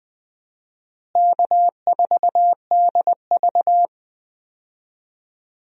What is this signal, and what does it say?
Morse code, heard as a keyed tone, about 20 wpm, 700 Hz. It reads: K4DV
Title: USA Call Signs - 4 Characters - Repeated 2x - 20wpm